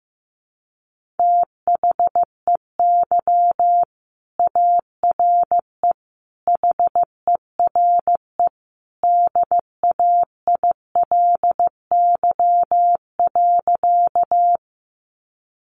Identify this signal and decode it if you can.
Morse code, heard as a keyed tone, about 15 wpm, 700 Hz. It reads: THEY ARE HERE DAILY.
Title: Sentences from Top 1100 Words 15wpm